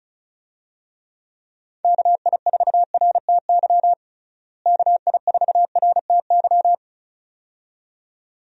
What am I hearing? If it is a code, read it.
Morse code, heard as a keyed tone, about 36 wpm, 700 Hz. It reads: KI4RTY KI4RTY
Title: USA Call Signs - 6 Characters - Repeated 2x - 35wpm